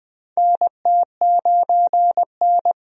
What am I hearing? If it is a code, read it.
Morse code, heard as a keyed tone, about 20 wpm, 700 Hz. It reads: NT9N
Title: USA Call Signs - 4 Characters - 20wpm